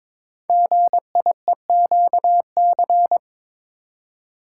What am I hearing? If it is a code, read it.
Morse code, heard as a keyed tone, about 22 wpm, 700 Hz. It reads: GIEQC
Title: Code Groups - 5 Letter:Numbers - 22wpm